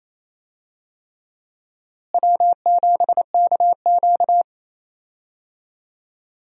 Morse code, heard as a keyed tone, about 28 wpm, 700 Hz. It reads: W7KQ
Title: USA Call Signs - 4 Characters - Repeated 3x - 28wpm